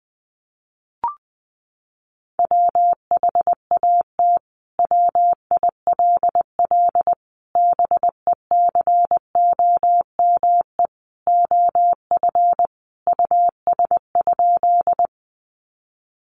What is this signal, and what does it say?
Morse code, heard as a keyed tone, about 20 wpm, 700 Hz. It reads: WHAT WILL BECOME OF US?
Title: Sentences from Top 600 Words 20wpm